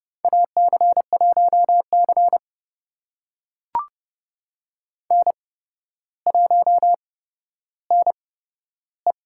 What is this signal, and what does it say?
Morse code, heard as a keyed tone, about 29 wpm, spaced as for 13 wpm, 700 Hz. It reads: AC1C N1NE
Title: USA Call Signs - 4 Characters - 8x Character Spacing - 30wpm